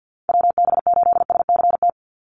Morse code, heard as a keyed tone, about 50 wpm, 700 Hz. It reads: WB8SCT
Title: Call Signs - 50wpm